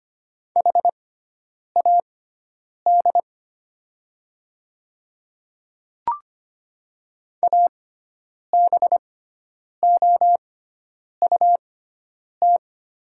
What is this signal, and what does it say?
Morse code, heard as a keyed tone, about 25 wpm, spaced as for 9 wpm, 700 Hz. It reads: HAD ABOUT
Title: 6x Character Spacing - Top 100 Words - 25wpm